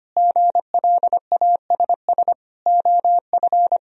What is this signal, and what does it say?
Morse code, heard as a keyed tone, about 25 wpm, 700 Hz. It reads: GLASS OF
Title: Sentences from Top 1000 Words - Review Part 3 - 25wpm